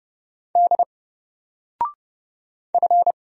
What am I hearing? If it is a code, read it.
Morse code, heard as a keyed tone, about 30 wpm, 700 Hz. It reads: D F
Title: Single Letter-Number Alt 30wpm